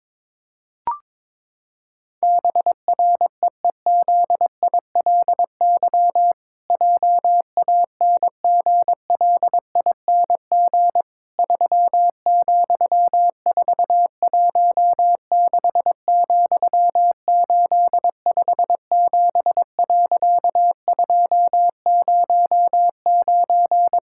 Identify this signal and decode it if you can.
Morse code, heard as a keyed tone, about 22 wpm, 700 Hz. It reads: BREEZILY JANGLING 3,416,857.209
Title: Pangram Challenge v2 - 22wpm